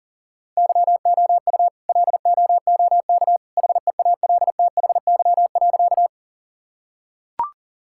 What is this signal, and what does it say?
Morse code, heard as a keyed tone, about 40 wpm, 700 Hz. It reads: YOU LOOK HEALTHY.
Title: Sentences from Top 1000 Words - Review Part 5 - 40wpm